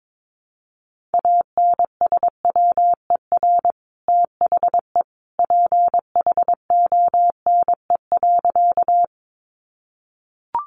Morse code, heard as a keyed tone, about 22 wpm, 700 Hz. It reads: ANSWER THE PHONE.
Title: Sentences from Top 1100 Words 22wpm